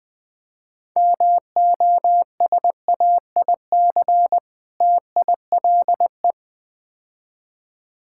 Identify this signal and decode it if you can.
Morse code, heard as a keyed tone, about 20 wpm, 700 Hz. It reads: MOSAIC TILE
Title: Difficult Words - Speed-Racing with Hints - 20wpm